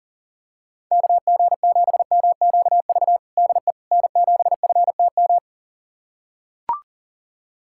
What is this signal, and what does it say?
Morse code, heard as a keyed tone, about 40 wpm, 700 Hz. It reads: KG7MQV DE N7FTM
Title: USA Callsign DE Callsign 40wpm